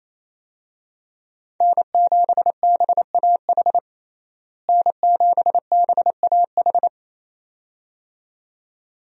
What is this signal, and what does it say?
Morse code, heard as a keyed tone, about 27 wpm, 700 Hz. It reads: N7BAH N7BAH
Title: USA Call Signs - 5 Characters - Repeated 2x - 28wpm